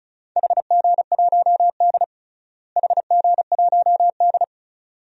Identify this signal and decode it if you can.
Morse code, heard as a keyed tone, about 34 wpm, 700 Hz. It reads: HG1D HG1D
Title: DXCC-100 Call Signs - 4 Characters - Repeated 2x - 35wpm